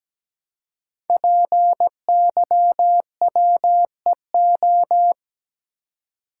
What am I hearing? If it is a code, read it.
Morse code, heard as a keyed tone, about 17 wpm, 700 Hz. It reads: PYWEO
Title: Code Groups - 5 Letters - 17wpm